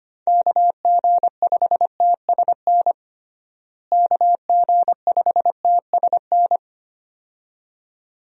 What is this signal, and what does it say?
Morse code, heard as a keyed tone, about 25 wpm, 700 Hz. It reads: KG5TSN KG5TSN
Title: USA Call Signs - 6 Characters - Repeated 2x - 25wpm